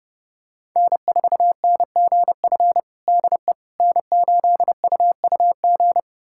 Morse code, heard as a keyed tone, about 29 wpm, 700 Hz. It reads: N4NGF DE N8UUG
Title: USA Callsign DE Callsign - 5 Characters - 30wpm